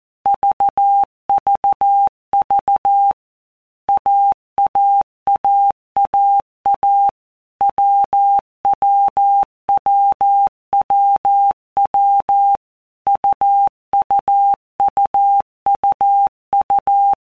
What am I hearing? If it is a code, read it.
Morse code, heard as a keyed tone, about 14 wpm, 800 Hz. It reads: VVV AAAAA WWWWW UUUUU
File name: Basic Letter Groups By Blocks / Groups@13 / AWUNDG@13